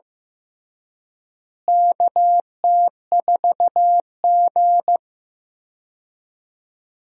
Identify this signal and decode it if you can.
Morse code, heard as a keyed tone, about 15 wpm, 700 Hz. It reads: KT4G
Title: USA Call Signs - 4 Characters - Repeated 3x - 15wpm